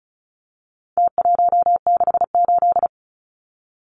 Morse code, heard as a keyed tone, about 34 wpm, 700 Hz. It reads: T168
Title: Numbers - Contest Style - 4-Digit 35wpm